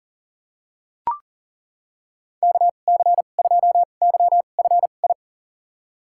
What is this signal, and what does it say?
Morse code, heard as a keyed tone, about 38 wpm, spaced as for 28 wpm, 700 Hz. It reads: KC2YFI
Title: USA Call Signs - 6 Characters - 2x Character Spacing - 40wpm